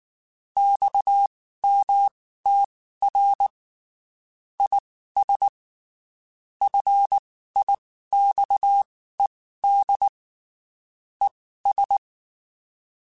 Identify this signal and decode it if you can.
Morse code, heard as a keyed tone, about 19 wpm, spaced as for 14 wpm, 800 Hz. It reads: XMTR IS FIXED ES
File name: (spaced) / QSOs Spaced-mp3s / WA7QOX 1 @18 sp